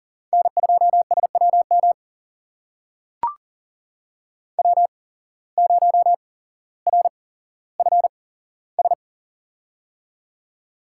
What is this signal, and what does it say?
Morse code, heard as a keyed tone, about 38 wpm, spaced as for 17 wpm, 700 Hz. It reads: N2SWM W0RFS
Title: USA Call Signs - 5 Characters - 8x Character Spacing - 40wpm